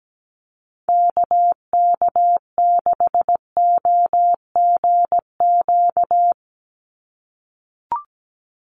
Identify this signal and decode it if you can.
Morse code, heard as a keyed tone, about 17 wpm, 700 Hz. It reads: KK6OGQ
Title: USA Call Signs 17wpm